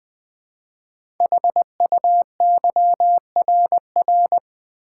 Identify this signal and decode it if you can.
Morse code, heard as a keyed tone, about 20 wpm, 700 Hz. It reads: HUYRR
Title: Code Groups - 5 Letters - 20wpm